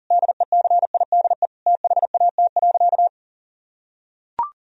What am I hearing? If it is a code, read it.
Morse code, heard as a keyed tone, about 37 wpm, 700 Hz. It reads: DECIDE THAT.
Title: Sentences from Top 1000 Words - Review Part 3 - 40wpm